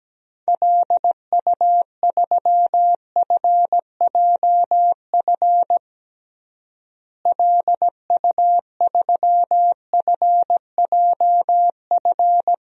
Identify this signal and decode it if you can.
Morse code, heard as a keyed tone, about 17 wpm, 700 Hz. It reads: LU3FJF LU3FJF
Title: DXCC-100 Call Signs - 6 Characters - Repeated 2x - 17wpm